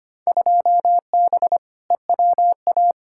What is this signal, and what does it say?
Morse code, heard as a keyed tone, about 25 wpm, 700 Hz. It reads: 2B EWA
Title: Field Day Exchanges Encore - 25wpm